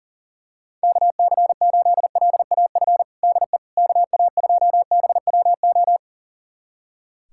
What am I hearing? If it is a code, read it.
Morse code, heard as a keyed tone, about 37 wpm, 700 Hz. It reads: KC8LAF DE KA2BWO
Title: USA Callsign DE Callsign - 6 Characters - 40wpm